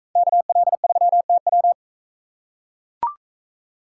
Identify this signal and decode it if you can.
Morse code, heard as a keyed tone, about 44 wpm, 700 Hz. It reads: KL3TW